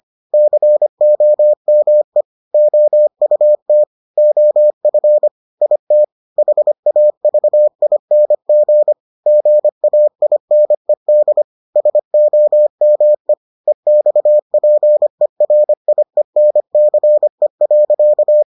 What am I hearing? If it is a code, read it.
Morse code, heard as a keyed tone, about 25 wpm, 600 Hz. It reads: COME OUT OF IT HAVING GAINED SOME EXPERIENCE.